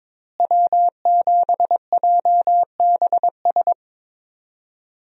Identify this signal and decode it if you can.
Morse code, heard as a keyed tone, about 22 wpm, 700 Hz. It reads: W7JBS
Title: USA Call Signs - 5 Characters - Repeated 2x - 22wpm